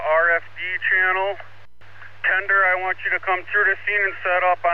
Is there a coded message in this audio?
Yes: Emergency Alert System (SAME) tones from 0.2 to 4.1 s, heard as harsh buzzing bursts, with speech.